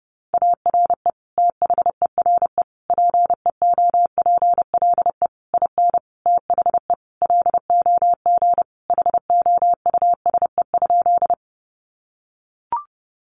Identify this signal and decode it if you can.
Morse code, heard as a keyed tone, about 30 wpm, 700 Hz. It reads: ARE THERE PEOPLE IN THE LOG HOUSE?
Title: Sentences from Top 1400 Words 30wpm